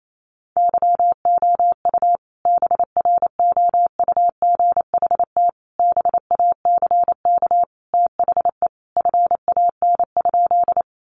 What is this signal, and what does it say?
Morse code, heard as a keyed tone, about 28 wpm, 700 Hz. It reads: YOU BROUGHT BACK THE FAN?
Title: Sentences from Top 1300 Words 28wpm